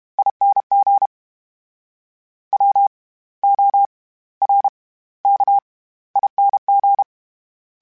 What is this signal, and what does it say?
Morse code, heard as a keyed tone, about 32 wpm, spaced as for 18 wpm, 800 Hz. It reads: ING WORKING
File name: Special INGs / Special INGs-mp3s / SlowToFaster ing @30